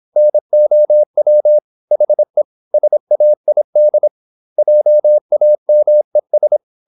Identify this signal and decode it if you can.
Morse code, heard as a keyed tone, about 26 wpm, 600 Hz. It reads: NOW HE SAID JAMES
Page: Never 26wpm | CW med Gnister